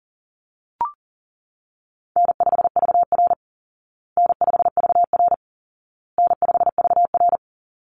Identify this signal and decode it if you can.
Morse code, heard as a keyed tone, about 40 wpm, 700 Hz. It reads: N5VR N5VR N5VR
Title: Easy Call Signs Repeated 3x - 40wpm